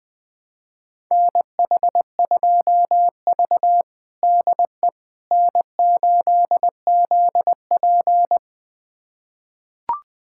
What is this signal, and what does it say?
Morse code, heard as a keyed tone, about 20 wpm, 700 Hz. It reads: NH2V DE N8ZP
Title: USA Callsign DE Callsign - 4 Characters - 20wpm